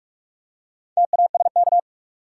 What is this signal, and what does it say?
Morse code, heard as a keyed tone, about 44 wpm, 700 Hz. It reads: TASK
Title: Four Letter Words - 45wpm